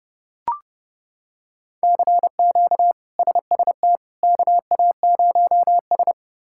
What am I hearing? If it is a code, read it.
Morse code, heard as a keyed tone, about 30 wpm, 700 Hz. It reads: CQ SST KA0S
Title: K1USN SST QSO Practice - 30wpm